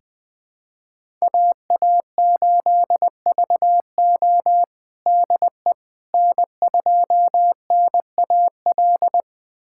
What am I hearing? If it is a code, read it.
Morse code, heard as a keyed tone, about 20 wpm, 700 Hz. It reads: AA8VO DE N2NAL
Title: USA Callsign DE Callsign - 5 Characters - 20wpm